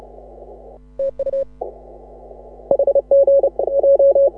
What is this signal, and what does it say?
Morse code, heard as a keyed tone, about 35 wpm, 530 Hz. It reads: TU HG J
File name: AUDIO2018 / HA6OI CW